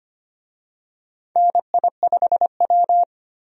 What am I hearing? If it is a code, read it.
Morse code, heard as a keyed tone, about 25 wpm, 700 Hz. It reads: NI5W
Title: USA Call Signs - 4 Characters - 25wpm